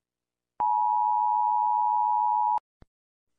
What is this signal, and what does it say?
Emergency Alert System (SAME) tones (the 853 and 960 Hz attention tone). It is heard from 0.6 to 2.6 s.